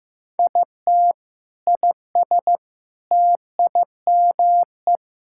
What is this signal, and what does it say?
Morse code, heard as a keyed tone, about 15 wpm, 700 Hz. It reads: IT IS TIME
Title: Sentences from Top 1600 Words 15wpm